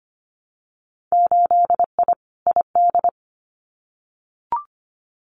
Morse code, heard as a keyed tone, about 25 wpm, 700 Hz. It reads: 8I ID
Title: Winter Field Day Exchanges Encore 25wpm